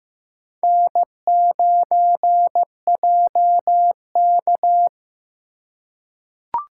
Morse code, heard as a keyed tone, about 15 wpm, 700 Hz. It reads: N9JK
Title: USA Call Signs - 4 Characters - 3x Character Spacing - 15wpm